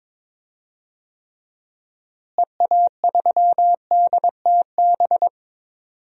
Morse code, heard as a keyed tone, about 22 wpm, 700 Hz. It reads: EA3DTB
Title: DXCC-100 Call Signs - 6 Characters - Repeated 3x - 22wpm